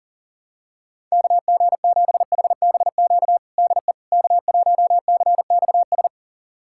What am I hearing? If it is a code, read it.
Morse code, heard as a keyed tone, about 38 wpm, 700 Hz. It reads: KG7HBQ DE K1CXS
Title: USA Callsign DE Callsign 40wpm